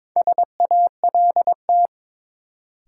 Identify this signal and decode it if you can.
Morse code, heard as a keyed tone, about 22 wpm, 700 Hz. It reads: SALT
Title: Sentences from Top 800 Words - New Word Drill - 22wpm